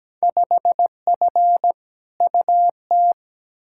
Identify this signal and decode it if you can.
Morse code, heard as a keyed tone, about 17 wpm, 700 Hz. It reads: 5F UT
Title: Field Day Exchanges - 17wpm